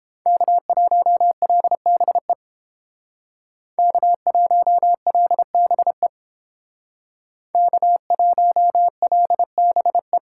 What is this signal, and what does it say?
Morse code, heard as a keyed tone, about 31 wpm, 700 Hz. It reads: K1LBE K1LBE K1LBE